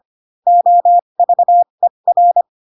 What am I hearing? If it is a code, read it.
Morse code, heard as a keyed tone, about 25 wpm, spaced as for 22 wpm, 700 Hz. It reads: OVER
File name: Pinocchio / Pinocchio ch4 18 25 ik1hge